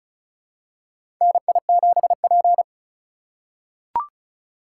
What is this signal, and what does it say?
Morse code, heard as a keyed tone, about 36 wpm, 700 Hz. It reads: NI7P
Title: USA Call Signs - 4 Characters - 4x Character Spacing - 35wpm